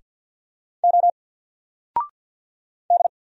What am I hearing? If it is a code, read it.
Morse code, heard as a keyed tone, about 48 wpm, 700 Hz. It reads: O D
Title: Single Letter-Number Alt 50wpm